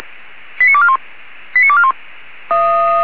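Five-tone selective call (ZVEI), heard as quick runs of five tones, tones of about 70 ms. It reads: 89241 89241
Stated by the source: ZVEI five-tone selective call received by SDR.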